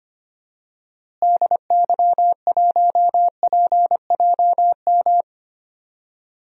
Morse code, heard as a keyed tone, about 25 wpm, 700 Hz. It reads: DY1PJM
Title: DXCC-100 Call Signs - 6 Characters - Repeated 3x - 25wpm